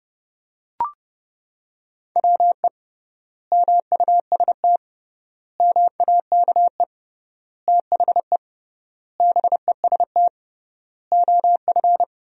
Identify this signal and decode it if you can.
Morse code, heard as a keyed tone, about 31 wpm, 700 Hz. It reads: WE MUST MAKE THE BEST OF
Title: 3x Word-Spacing - Sentences from Top 400 Words - 30wpm